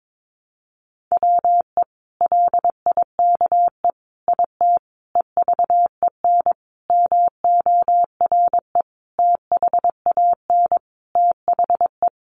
Morse code, heard as a keyed tone, about 22 wpm, 700 Hz. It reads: WE LIKE IT EVEN MORE THAN THE
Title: Sentences from Top 1100 Words 22wpm